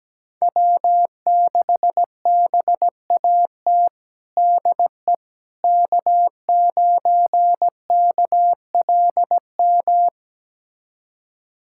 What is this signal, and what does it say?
Morse code, heard as a keyed tone, about 17 wpm, 700 Hz. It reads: W6BAT DE K9KLM
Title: USA Callsign DE Callsign - 5 Characters - 17wpm